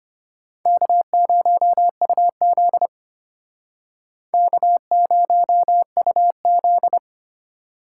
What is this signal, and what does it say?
Morse code, heard as a keyed tone, about 32 wpm, 700 Hz. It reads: K0UZ K0UZ